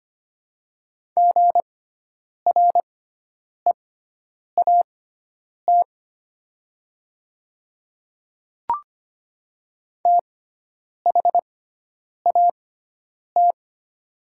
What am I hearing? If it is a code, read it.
Morse code, heard as a keyed tone, about 25 wpm, spaced as for 9 wpm, 700 Hz. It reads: GREAT THAT
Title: 6x Character Spacing - Top 300 Words - 25wpm